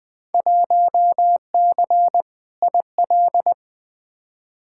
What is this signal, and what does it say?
Morse code, heard as a keyed tone, about 20 wpm, 700 Hz. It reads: 1C IL
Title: Field Day Exchanges - 20wpm